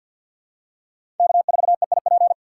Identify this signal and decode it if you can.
Morse code, heard as a keyed tone, about 52 wpm, 700 Hz. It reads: K4EIP
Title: USA Call Signs - 5 Characters - 2x Character Spacing - 50wpm